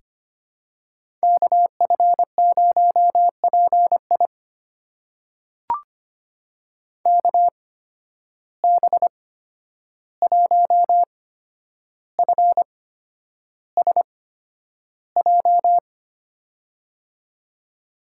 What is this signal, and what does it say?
Morse code, heard as a keyed tone, about 25 wpm, spaced as for 10 wpm, 700 Hz. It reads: KF0PI KB1FSJ
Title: USA Call Signs - 8x Character Spacing - 25wpm